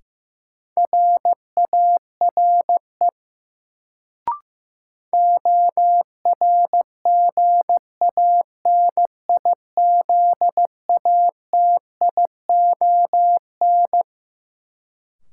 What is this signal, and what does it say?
Morse code, heard as a keyed tone, about 15 wpm, 700 Hz. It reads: RARE ORGANIZATION
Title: Sentences from Top 1100 Words - New Word Drill Alt - 15wpm